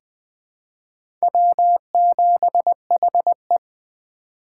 Morse code, heard as a keyed tone, about 20 wpm, 700 Hz. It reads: W7HE
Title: USA Call Signs - 4 Characters - 2x Character Spacing - 20wpm